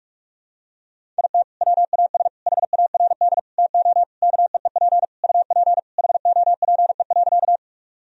Morse code, heard as a keyed tone, about 41 wpm, 700 Hz. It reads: IT WAS HARD TO KEEP UP HOPE.
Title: Sentences from Top 1000 Words - Review Part 1 - 45wpm